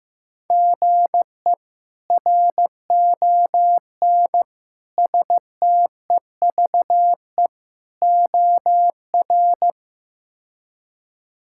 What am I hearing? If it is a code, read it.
Morse code, heard as a keyed tone, about 16 wpm, 700 Hz. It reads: GE RON STEVE OR
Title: K1USN SST QSO Practice - 15wpm